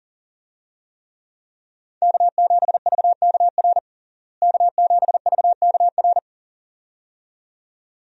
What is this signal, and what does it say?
Morse code, heard as a keyed tone, about 40 wpm, 700 Hz. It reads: K7VKR K7VKR
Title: USA Call Signs - 5 Characters - Repeated 2x - 40wpm